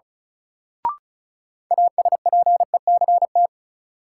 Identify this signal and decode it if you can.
Morse code, heard as a keyed tone, about 36 wpm, 700 Hz. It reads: ASPECT
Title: Sentences from Top 1300 Words - New Word Drill Alt - 35wpm